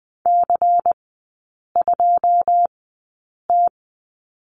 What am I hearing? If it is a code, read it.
Morse code, heard as a keyed tone, about 20 wpm, spaced as for 8 wpm, 700 Hz. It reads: C2T
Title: 138 Lesson 23e - Letter f - Three Characters 20wpm